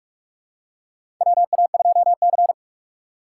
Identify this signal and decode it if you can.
Morse code, heard as a keyed tone, about 46 wpm, 700 Hz. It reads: WA2C
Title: USA Call Signs - 4 Characters - 6x Character Spacing - 45wpm